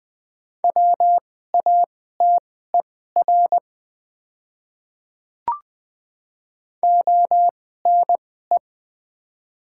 Morse code, heard as a keyed tone, about 20 wpm, spaced as for 15 wpm, 700 Hz. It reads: WATER ONE
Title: 2x Character Spacing - Top 100 Words - 20wpm